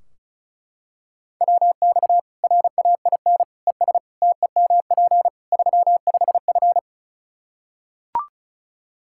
Morse code, heard as a keyed tone, about 36 wpm, 700 Hz. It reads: WX RAIN ES TEMP 35F
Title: Practice QSO - Exchange 2 - Info - 35wpm